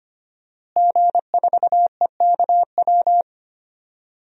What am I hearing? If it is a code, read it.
Morse code, heard as a keyed tone, about 25 wpm, 700 Hz. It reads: G4EKW
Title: DXCC-100 Call Signs - 5 Characters - Repeated 3x - 25wpm